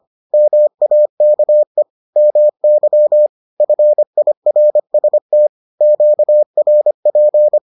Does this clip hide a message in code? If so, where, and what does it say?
Yes: Morse code from 0.3 to 7.6 s, heard as a keyed tone, about 25 wpm, 600 Hz. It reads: MAKE MY FIRST QRP